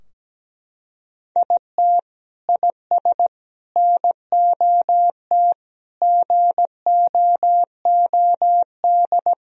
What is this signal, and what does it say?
Morse code, heard as a keyed tone, about 17 wpm, 700 Hz. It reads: IT IS NOT GOOD
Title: Sentences from Top 1000 Words - Review Part 4 - 17wpm